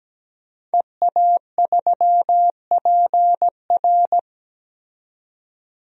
Morse code, heard as a keyed tone, about 17 wpm, 700 Hz. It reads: EA3PR
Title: DXCC-100 Call Signs - 5 Characters - 17wpm